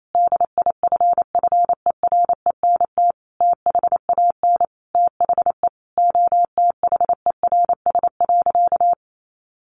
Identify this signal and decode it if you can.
Morse code, heard as a keyed tone, about 28 wpm, 700 Hz. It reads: DIFFERENT THAN THE OTHERS.
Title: Sentences from Top 1500 Words 28wpm